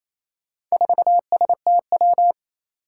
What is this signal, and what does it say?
Morse code, heard as a keyed tone, about 28 wpm, 700 Hz. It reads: 4STW
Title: Four Challenging Letter-Number Combinations 28wpm